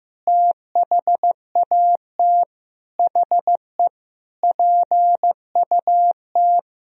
Morse code, heard as a keyed tone, about 15 wpm, 700 Hz. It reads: THAT HE PUT
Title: Sentences from Top 1000 Words - Review Part 3 - 15wpm